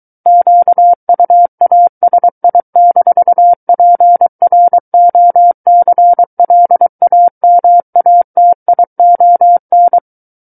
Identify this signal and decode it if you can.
Morse code, heard as a keyed tone, about 23 wpm, 700 Hz. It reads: QUASI-PROCLAMATION
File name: Book One / 14-Chapter 14 - 23 WPM